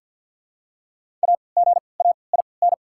Morse code, heard as a keyed tone, about 48 wpm, spaced as for 28 wpm, 700 Hz. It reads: AGAIN
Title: 3x Character Spacing - Top 200 Words - 50wpm